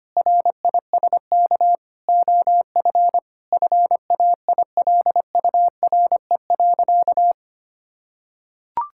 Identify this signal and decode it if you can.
Morse code, heard as a keyed tone, about 25 wpm, 700 Hz. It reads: RISK OF FAILURE.
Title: Sentences from Top 1000 Words - Review Part 6 - 25wpm